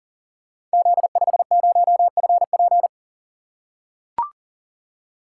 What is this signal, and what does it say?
Morse code, heard as a keyed tone, about 38 wpm, 700 Hz. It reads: Z50FP
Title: Code Groups - 5 Letter:Numbers - 40wpm